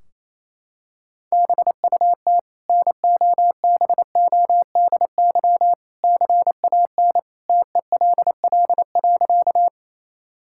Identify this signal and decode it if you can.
Morse code, heard as a keyed tone, about 28 wpm, 700 Hz. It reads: BUT NOBODY CAN TELL.
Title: Sentences from Top 1600 Words 28wpm